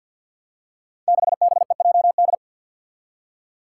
Morse code, heard as a keyed tone, about 52 wpm, 700 Hz. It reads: 6BEJD
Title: Five Challenging Letter-Number Combinations 50wpm